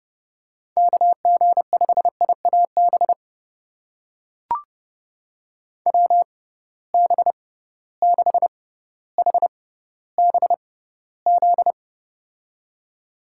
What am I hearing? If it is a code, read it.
Morse code, heard as a keyed tone, about 30 wpm, spaced as for 15 wpm, 700 Hz. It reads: KG5IAB WB6HBZ
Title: USA Call Signs - 6 Characters - 6x Character Spacing - 30wpm